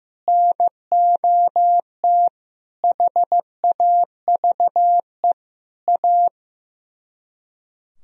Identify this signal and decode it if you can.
Morse code, heard as a keyed tone, about 15 wpm, 700 Hz. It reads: NOT HAVE A
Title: Speed-Racing - Sets of 5 Words 15wpm